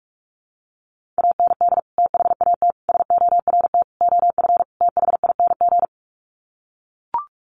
Morse code, heard as a keyed tone, about 44 wpm, 700 Hz. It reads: AND THAT SORT OF THING
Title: Sets of 5 Words Encore2 45wpm